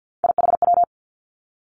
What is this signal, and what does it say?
Morse code, heard as a keyed tone, about 50 wpm, 700 Hz. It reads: IHW
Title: Three Challenging Letter-Number Combinations 50wpm